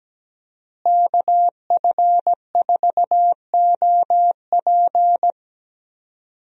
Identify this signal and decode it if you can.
Morse code, heard as a keyed tone, about 17 wpm, 700 Hz. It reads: KF4OP
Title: USA Call Signs - 5 Characters - Repeated 3x - 17wpm